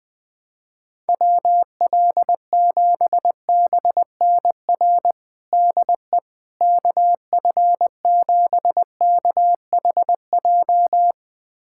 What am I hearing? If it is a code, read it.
Morse code, heard as a keyed tone, about 20 wpm, 700 Hz. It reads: WL7BNR DE KF7KHJ
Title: USA Callsign DE Callsign - 6 Characters - 20wpm